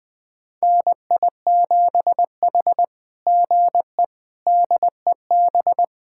Morse code, heard as a keyed tone, about 20 wpm, 700 Hz. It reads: NI7H GE DEB
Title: SOTA QSO Practice 20wpm